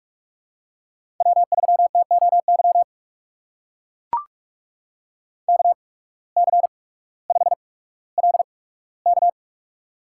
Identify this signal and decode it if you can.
Morse code, heard as a keyed tone, about 42 wpm, spaced as for 19 wpm, 700 Hz. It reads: W3TOY KC5LK
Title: USA Call Signs - 8x Character Spacing - 45wpm